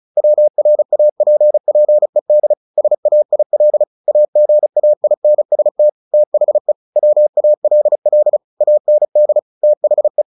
Morse code, heard as a keyed tone, about 34 wpm, 600 Hz. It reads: WRAPPED SAIL AGAINST THE WALL AND THE
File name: the old man and the sea / the old man and the sea 1 35wpm